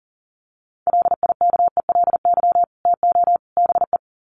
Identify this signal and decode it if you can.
Morse code, heard as a keyed tone, about 40 wpm, 700 Hz. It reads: LIKELY TO BE
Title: Sets of 3 Words 40wpm